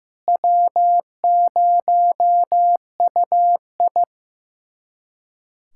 Morse code, heard as a keyed tone, about 15 wpm, 700 Hz. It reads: W0UI
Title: USA Callsign DE Callsign - 4 Characters - 15wpm